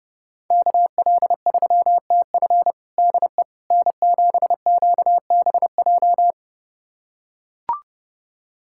Morse code, heard as a keyed tone, about 30 wpm, 700 Hz. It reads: KL3TF DE N7QBJ
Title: USA Callsign DE Callsign - 5 Characters - 30wpm